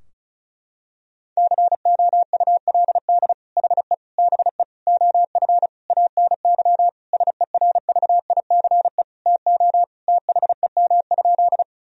Morse code, heard as a keyed tone, about 36 wpm, 700 Hz. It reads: COULD HE BE OF ANY SERVICE TO THEM?
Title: Sentences from Top 1000 Words - Review Part 4 - 35wpm